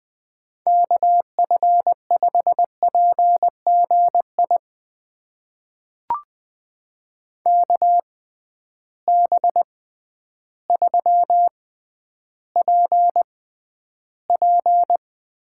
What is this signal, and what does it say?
Morse code, heard as a keyed tone, about 20 wpm, spaced as for 11 wpm, 700 Hz. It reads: KF5PGI KB3PP
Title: USA Call Signs - 6 Characters - 6x Character Spacing - 20wpm